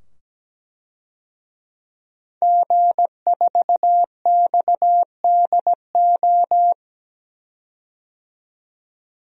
Morse code, heard as a keyed tone, about 17 wpm, 700 Hz. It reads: G4XDO
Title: DXCC-100 Call Signs - 5 Characters - 17wpm